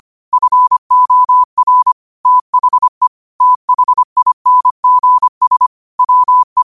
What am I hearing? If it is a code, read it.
Morse code, heard as a keyed tone, about 25 wpm, 1000 Hz. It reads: FOR THE THINGS WE
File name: qotdfiles / 25wpm